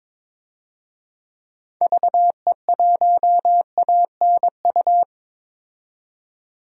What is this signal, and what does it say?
Morse code, heard as a keyed tone, about 22 wpm, 700 Hz. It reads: VE1ANU
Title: DXCC-100 Call Signs - 6 Characters - 22wpm